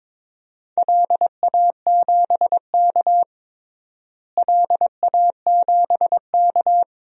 Morse code, heard as a keyed tone, about 22 wpm, 700 Hz. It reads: LA7K LA7K
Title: DXCC-100 Call Signs - 4 Characters - Repeated 2x - 22wpm